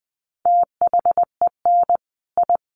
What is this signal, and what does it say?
Morse code, heard as a keyed tone, about 20 wpm, 700 Hz. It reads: THEN I
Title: A Sound Of Thunder 20wpm